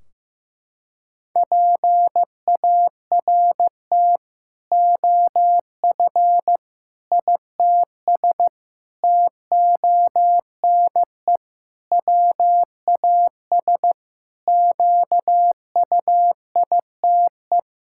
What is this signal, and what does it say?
Morse code, heard as a keyed tone, about 15 wpm, 700 Hz. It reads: PART OF ITS TONE WAS QUITE
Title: Sentences from Top 1600 Words 15wpm